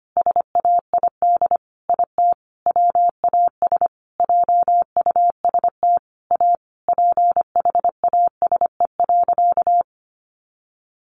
Morse code, heard as a keyed tone, about 25 wpm, 700 Hz. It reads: SAID IT WAS JUST A PHASE.
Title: Sentences from Top 1200 Words 25wpm